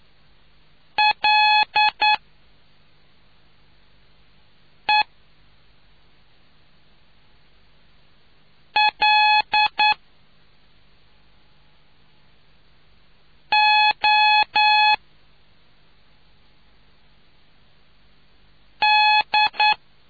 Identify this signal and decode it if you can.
Morse code, heard as a keyed tone, about 9 wpm, spaced as for 2 wpm, 850 Hz. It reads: LELOD